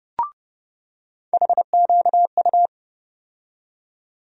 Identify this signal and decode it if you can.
Morse code, heard as a keyed tone, about 30 wpm, 700 Hz. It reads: HQU
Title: USA Call Sign Postfixes - 3 Letters 30wpm